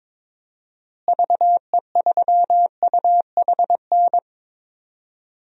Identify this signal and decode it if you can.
Morse code, heard as a keyed tone, about 22 wpm, 700 Hz. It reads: VE3UHN
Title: DXCC-100 Call Signs - 6 Characters - Repeated 2x - 22wpm